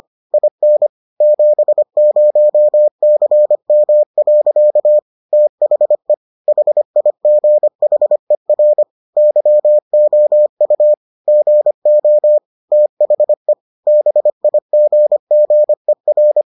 Morse code, heard as a keyed tone, about 25 wpm, 600 Hz. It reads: IN 70CM. THE HIGHER YOU GO THE BIGGER